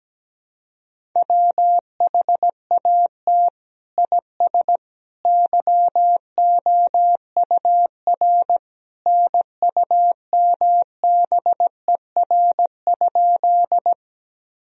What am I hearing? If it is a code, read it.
Morse code, heard as a keyed tone, about 17 wpm, 700 Hz. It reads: WHAT IS YOUR NUMBER?
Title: Sentences from Top 1000 Words - Review Part 1 - 17wpm